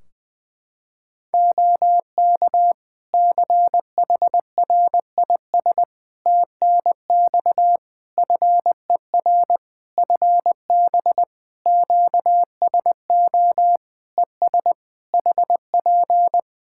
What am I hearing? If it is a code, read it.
Morse code, heard as a keyed tone, about 20 wpm, 700 Hz. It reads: OK CHRIS TNX FER FB QSO ES HP
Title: Practice QSO - Exchange 4 - Ending - 20wpm